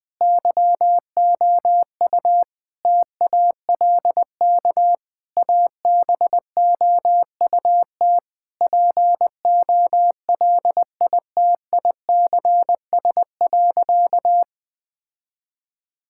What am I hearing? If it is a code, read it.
Morse code, heard as a keyed tone, about 20 wpm, 700 Hz. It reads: YOU TALK ABOUT POLITICS.
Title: Sentences from Top 1000 Words - Review Part 6 - 20wpm